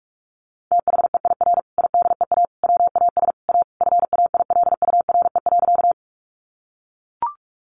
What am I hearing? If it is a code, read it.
Morse code, heard as a keyed tone, about 44 wpm, 700 Hz. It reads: THEIR IDEA WAS A FAILURE.
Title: Sentences from Top 900 Words 45wpm